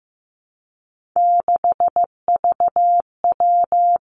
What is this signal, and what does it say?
Morse code, heard as a keyed tone, about 15 wpm, 700 Hz. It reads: 6VW
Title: Three Challenging Letter-Number Combinations 15wpm